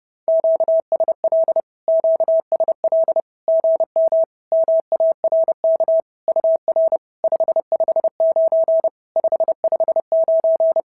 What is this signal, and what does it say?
Morse code, heard as a keyed tone, about 30 wpm, 650 Hz. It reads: QSL QSL GM MARK UR 559 559
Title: SOTA QSO Practice 30wpm